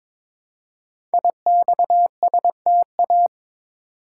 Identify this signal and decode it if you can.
Morse code, heard as a keyed tone, about 22 wpm, 700 Hz. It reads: IXSTA
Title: Code Groups - 5 Letters - 22wpm